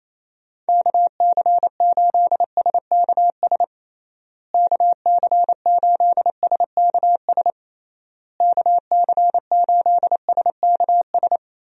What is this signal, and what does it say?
Morse code, heard as a keyed tone, about 28 wpm, 700 Hz. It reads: KC8SKS KC8SKS KC8SKS
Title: USA Call Signs - 6 Characters - Repeated 3x - 28wpm